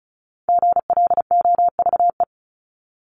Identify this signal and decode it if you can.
Morse code, heard as a keyed tone, about 34 wpm, 700 Hz. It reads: GLOVE
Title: Top 1751-2000 Words 35wpm